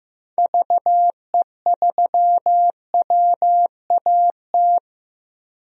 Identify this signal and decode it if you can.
Morse code, heard as a keyed tone, about 15 wpm, 700 Hz. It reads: VE3WAT
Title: DXCC-100 Call Signs - 6 Characters - 15wpm